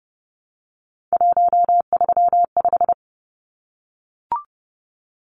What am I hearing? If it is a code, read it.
Morse code, heard as a keyed tone, about 30 wpm, 700 Hz. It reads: 135
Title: Numbers - 3-Digit 30wpm